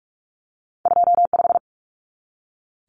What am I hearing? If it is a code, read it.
Morse code, heard as a keyed tone, about 46 wpm, 700 Hz. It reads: 25
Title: Common Cut Numbers - 2-Digit 45wpm